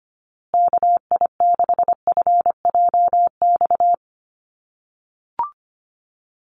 Morse code, heard as a keyed tone, about 25 wpm, 700 Hz. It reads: KI6FJX
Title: USA Call Signs Repeated 3x 25wpm